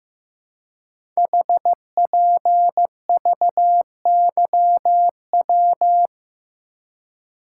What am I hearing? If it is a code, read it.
Morse code, heard as a keyed tone, about 15 wpm, 700 Hz. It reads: HPVYW
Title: Code Groups - 5 Letters - 15wpm